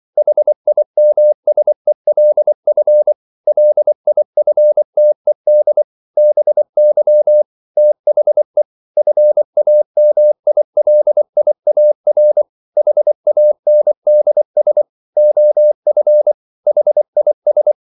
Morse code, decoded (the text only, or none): HIMSELF LIFTED BY THE FAMILIAR HANDS OF HIS